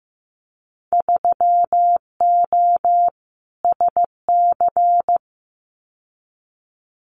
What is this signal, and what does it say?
Morse code, heard as a keyed tone, about 15 wpm, 700 Hz. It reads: 3O SC
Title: Winter Field Day Exchanges Encore 15wpm